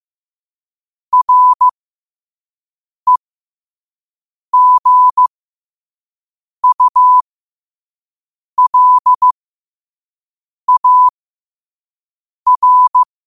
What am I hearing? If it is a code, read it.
Morse code, heard as a keyed tone, about 15 wpm, spaced as for 5 wpm, 1000 Hz. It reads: REGULAR
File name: qotdfiles / 5wpm